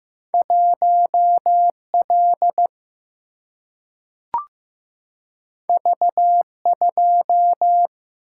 Morse code, heard as a keyed tone, about 15 wpm, 700 Hz. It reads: 1L V2
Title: Single-Letter Single-Number 15wpm